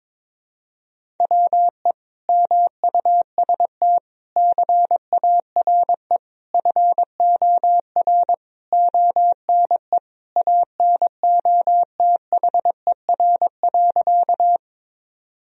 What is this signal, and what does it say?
Morse code, heard as a keyed tone, about 22 wpm, 700 Hz. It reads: WE MUST CARE FOR ONE ANOTHER.
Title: Sentences from Top 1000 Words - Review Part 1 - 22wpm